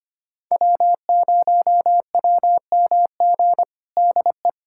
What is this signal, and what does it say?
Morse code, heard as a keyed tone, about 25 wpm, 700 Hz. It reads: W0WMG DE
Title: USA Callsign DE Callsign - 5 Characters - 25wpm